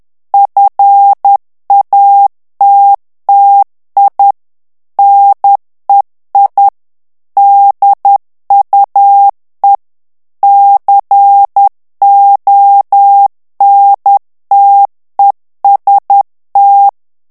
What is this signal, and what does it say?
Morse code, heard as a keyed tone, about 10 wpm, 800 Hz. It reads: FATTI NEI DUE CONTEST
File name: ITALIEN texte clair 2 vitesse 10 mots minute